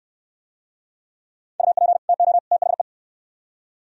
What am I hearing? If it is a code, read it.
Morse code, heard as a keyed tone, about 63 wpm, 700 Hz. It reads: UP TO THE